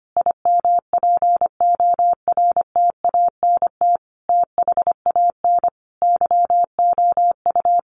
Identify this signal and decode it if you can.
Morse code, heard as a keyed tone, about 25 wpm, 700 Hz. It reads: IMPORTANT THAN YOU
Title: Sentences from Top 600 Words 25wpm